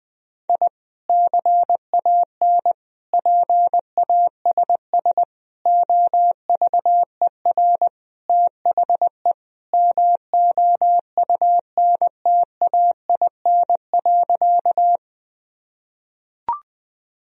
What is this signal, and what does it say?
Morse code, heard as a keyed tone, about 20 wpm, 700 Hz. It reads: I CAN PASS OVER THE MOUNTAIN.
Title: Sentences from Top 1000 Words - Review Part 1 - 20wpm